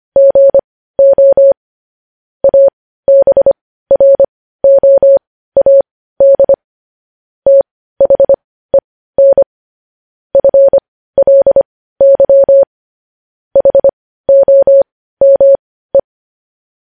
Morse code, heard as a keyed tone, about 25 wpm, spaced as for 15 wpm, 550 Hz. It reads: GO ABROAD THEN FLY HOME
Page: news headlines in morse code at 15 wpm